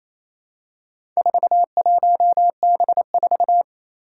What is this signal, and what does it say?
Morse code, heard as a keyed tone, about 28 wpm, 700 Hz. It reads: 41B4
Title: Four Challenging Letter-Number Combinations 28wpm